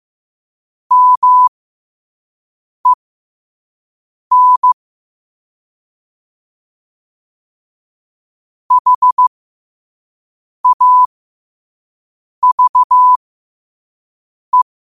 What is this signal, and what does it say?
Morse code, heard as a keyed tone, about 15 wpm, spaced as for 5 wpm, 1000 Hz. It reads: MEN HAVE
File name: qotdfiles / 5wpm